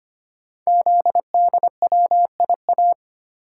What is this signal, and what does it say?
Morse code, heard as a keyed tone, about 25 wpm, 700 Hz. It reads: ZDWIA
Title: Code Groups - 5 Letters - 25wpm